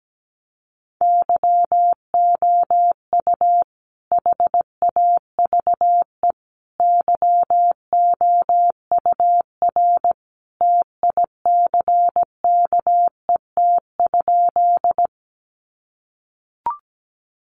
Morse code, heard as a keyed tone, about 17 wpm, 700 Hz. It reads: YOU HAVE YOUR TICKET?
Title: Sentences from Top 1400 Words 17wpm